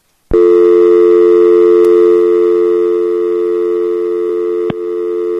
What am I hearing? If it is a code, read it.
Dial tone (North America), 350 + 440 Hz.